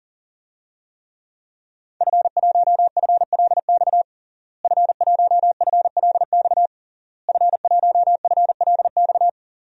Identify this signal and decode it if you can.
Morse code, heard as a keyed tone, about 40 wpm, 700 Hz. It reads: F1FLX F1FLX F1FLX
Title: DXCC-100 Call Signs - 5 Characters - Repeated 3x - 40wpm